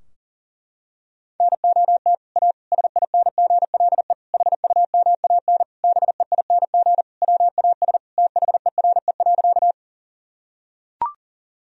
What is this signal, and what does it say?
Morse code, heard as a keyed tone, about 37 wpm, 700 Hz. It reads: NOT A SINGLE HUMAN BEING WAS THERE.
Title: Sentences from Top 1000 Words - Review Part 4 - 40wpm